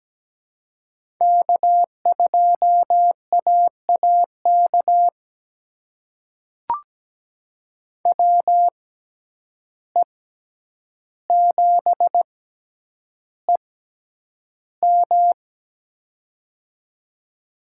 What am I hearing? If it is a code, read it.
Morse code, heard as a keyed tone, about 17 wpm, spaced as for 9 wpm, 700 Hz. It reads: K2AAK WE7EM
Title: USA Call Signs - 5 Characters - 6x Character Spacing - 17wpm